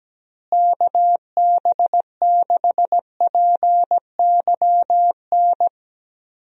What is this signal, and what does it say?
Morse code, heard as a keyed tone, about 17 wpm, 700 Hz. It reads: KB6PYN
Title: USA Call Signs - 6 Characters - Repeated 2x - 17wpm